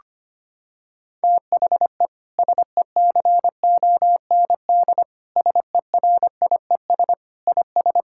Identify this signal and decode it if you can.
Morse code, heard as a keyed tone, about 25 wpm, 700 Hz. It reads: THE SECOND SERIES IS
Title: Sentences from Top 1000 Words - Review Part 5 - 25wpm